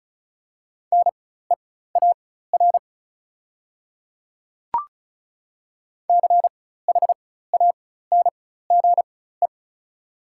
Morse code, heard as a keyed tone, about 36 wpm, spaced as for 16 wpm, 690 Hz. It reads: NEAR CHANGE
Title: 4x Character Spacing - Top 200 Words - 35wpm